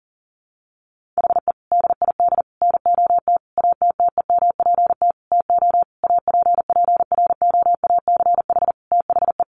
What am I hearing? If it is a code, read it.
Morse code, heard as a keyed tone, about 40 wpm, 700 Hz. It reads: HE DID NOT ATTEMPT TO APPROACH THE
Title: Sentences from Top 1400 Words 40wpm